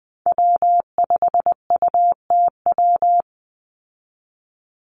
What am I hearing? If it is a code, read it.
Morse code, heard as a keyed tone, about 20 wpm, 700 Hz. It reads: W5UTW
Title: Call Signs Repeated 2x - 20wpm